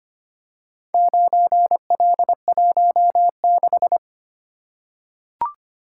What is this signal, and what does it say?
Morse code, heard as a keyed tone, about 25 wpm, 700 Hz. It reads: 9L16
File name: practice / 25wpm / Four Letter-Number Combinations - 25wpm